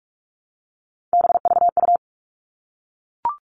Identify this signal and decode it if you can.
Morse code, heard as a keyed tone, about 44 wpm, 700 Hz. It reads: BVU
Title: All Cut Numbers - 3-Digit 45wpm